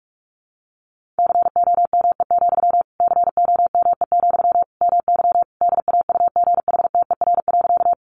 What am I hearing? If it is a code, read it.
Morse code, heard as a keyed tone, about 44 wpm, 700 Hz. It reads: COME, COME, MY DAUGHTER.
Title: Sentences from Top 1500 Words 45wpm